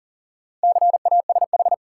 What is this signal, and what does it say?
Morse code, heard as a keyed tone, about 40 wpm, 700 Hz. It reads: CASH
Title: Sentences from Top 700 Words - New Word Drill - 40wpm